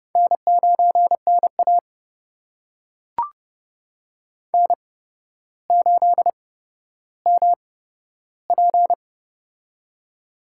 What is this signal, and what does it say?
Morse code, heard as a keyed tone, about 29 wpm, spaced as for 13 wpm, 700 Hz. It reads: N9NA N8MP
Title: USA Call Signs - 4 Characters - 8x Character Spacing - 30wpm